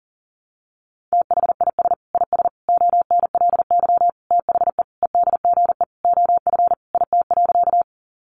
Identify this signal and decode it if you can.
Morse code, heard as a keyed tone, about 40 wpm, 700 Hz. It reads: THIS IS ONLY THE EDGE OF IT.
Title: Sentences from Top 1300 Words 40wpm